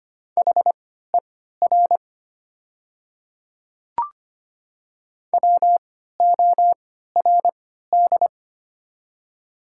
Morse code, heard as a keyed tone, about 25 wpm, spaced as for 14 wpm, 700 Hz. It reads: HER WORD
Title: 3x Character Spacing - Top 100 Words - 25wpm